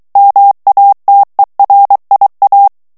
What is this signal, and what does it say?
Morse code, heard as a keyed tone, about 23 wpm, 800 Hz. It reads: MATERIA
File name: morse mp3 / ESPAGNOL TEXTE EN CLAIR 2 VITESSE 22 MOTS MINUTES